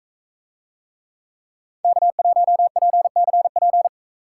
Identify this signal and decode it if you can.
Morse code, heard as a keyed tone, about 42 wpm, 700 Hz. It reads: K1PCP